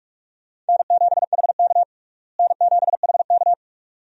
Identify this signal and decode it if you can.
Morse code, heard as a keyed tone, about 46 wpm, 700 Hz. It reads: N7HK N7HK
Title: USA Call Signs - 4 Characters - Repeated 2x - 45wpm